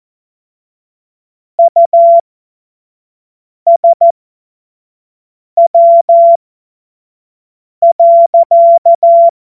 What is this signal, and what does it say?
Morse code, heard as a keyed tone, about 14 wpm, spaced as for 5 wpm, 680 Hz. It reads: USW.
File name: Morsekurs / Witze / Witz 15. August 2023 25 F70